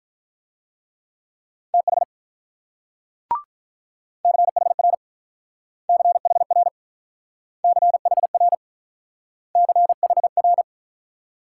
Morse code, heard as a keyed tone, about 45 wpm, 700 Hz. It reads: TH CHR CHR CHR CHR